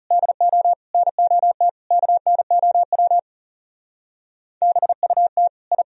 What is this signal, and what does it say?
Morse code, heard as a keyed tone, about 39 wpm, 700 Hz. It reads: DO NOT KNOW BUT I